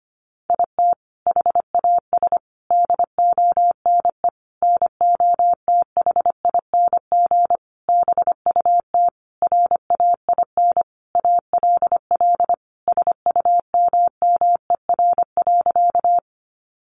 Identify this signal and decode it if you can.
Morse code, heard as a keyed tone, about 25 wpm, 700 Hz. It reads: IT HAS DONE NOTHING BUT RAIN ALL SUMMER.
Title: Sentences from Top 1100 Words 25wpm